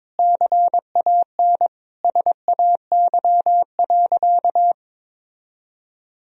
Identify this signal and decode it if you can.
Morse code, heard as a keyed tone, about 22 wpm, 700 Hz. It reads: CAN SAY.
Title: Sentences from Top 1000 Words - Review Part 2 - 22wpm